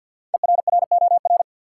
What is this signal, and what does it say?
Morse code, heard as a keyed tone, about 52 wpm, 700 Hz. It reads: ERROR
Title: Five Letter Words - 50wpm